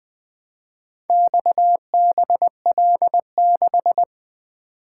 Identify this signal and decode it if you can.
Morse code, heard as a keyed tone, about 20 wpm, 700 Hz. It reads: XBL6
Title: Four Letter-Number Combinations - 20wpm